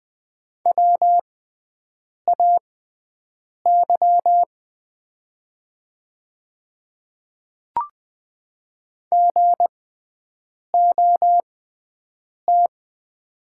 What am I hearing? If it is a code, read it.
Morse code, heard as a keyed tone, about 20 wpm, spaced as for 7 wpm, 700 Hz. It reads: WAY GOT
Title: 6x Character Spacing - Top 300 Words - 20wpm